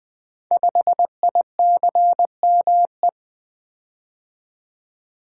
Morse code, heard as a keyed tone, about 20 wpm, 700 Hz. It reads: 5ICME
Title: Code Groups - 5 Letter:Numbers - 20wpm